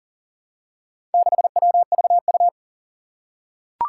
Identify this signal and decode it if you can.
Morse code, heard as a keyed tone, about 40 wpm, 700 Hz. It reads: 6WVU
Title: Four Challenging Letter-Number Combinations 40wpm